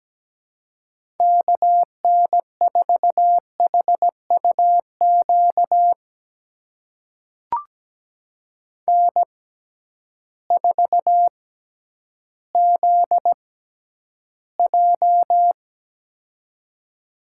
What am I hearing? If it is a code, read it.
Morse code, heard as a keyed tone, about 17 wpm, spaced as for 10 wpm, 700 Hz. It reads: KN4HUQ N4ZJ
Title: USA Call Signs - 6x Character Spacing - 17wpm